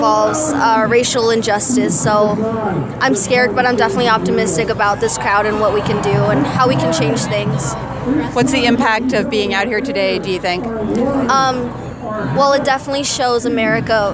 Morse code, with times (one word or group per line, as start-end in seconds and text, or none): none